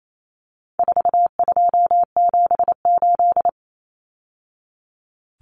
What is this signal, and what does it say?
Morse code, heard as a keyed tone, about 28 wpm, 700 Hz. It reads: 4278
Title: Numbers - 4-Digit 28wpm